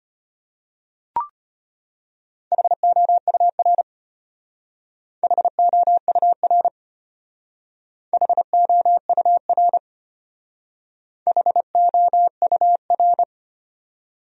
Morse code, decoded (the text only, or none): HOUR HOUR HOUR HOUR